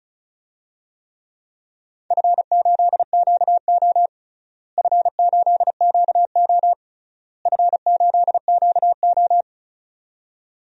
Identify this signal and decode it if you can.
Morse code, heard as a keyed tone, about 34 wpm, 700 Hz. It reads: F8QO F8QO F8QO
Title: DXCC-100 Call Signs - 4 Characters - Repeated 3x - 35wpm